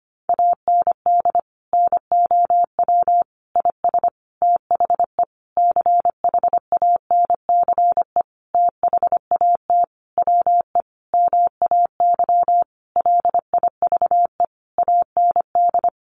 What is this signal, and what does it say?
Morse code, heard as a keyed tone, about 25 wpm, 700 Hz. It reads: AND NOW IS THE CHANCE THAT WE MAY LIVE AND
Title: Sentences from Top 800 Words 25wpm